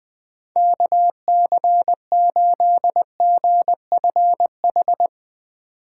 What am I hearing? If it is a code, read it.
Morse code, heard as a keyed tone, about 20 wpm, 700 Hz. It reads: KC8GFH
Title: Speed Racing - USA 2x3 Call Signs 17wpm